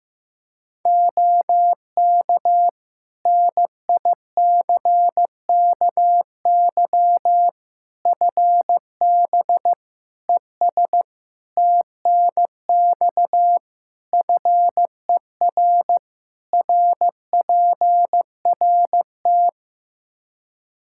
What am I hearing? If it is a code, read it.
Morse code, heard as a keyed tone, about 15 wpm, 700 Hz. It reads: OK NICKY FB ES TNX FER RPRT
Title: Practice QSO - Exchange 2 - Info - 15wpm